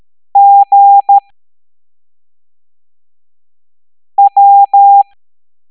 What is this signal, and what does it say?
Morse code, heard as a keyed tone, about 13 wpm, spaced as for 3 wpm, 800 Hz. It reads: GW